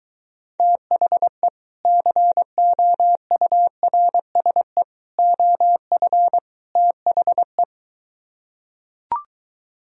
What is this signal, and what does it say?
Morse code, heard as a keyed tone, about 23 wpm, 700 Hz. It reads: THE COURSE OF THE